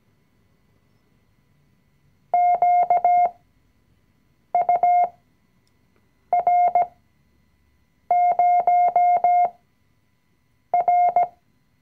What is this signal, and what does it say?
Morse code, heard as a keyed tone, about 17 wpm, spaced as for 6 wpm, 710 Hz. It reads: QUR0R
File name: CW-Online-Kurs / Anfänger / Lektion 020